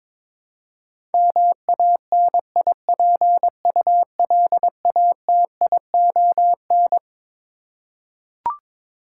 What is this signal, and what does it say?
Morse code, heard as a keyed tone, about 21 wpm, 700 Hz. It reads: MANIPULATION
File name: Twelve Letter Words - 22wpm